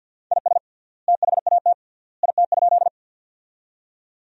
Morse code, heard as a keyed tone, about 50 wpm, 700 Hz. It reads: IS THAT IT?
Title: 3x Word-Spacing - Sentences from Top 100 Words - 50wpm